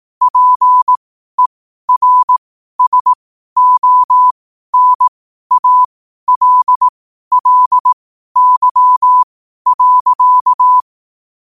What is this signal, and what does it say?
Morse code, heard as a keyed tone, about 18 wpm, spaced as for 12 wpm, 1000 Hz. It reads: PERSONALLY.
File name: qotdfiles / 12wpm